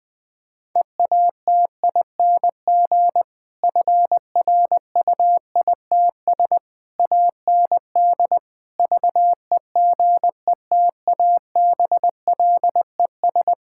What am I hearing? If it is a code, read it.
Morse code, heard as a keyed tone, about 20 wpm, 700 Hz. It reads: EATING FRUITS AND VEGETABLES
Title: Difficult Sentences with Hints - 20wpm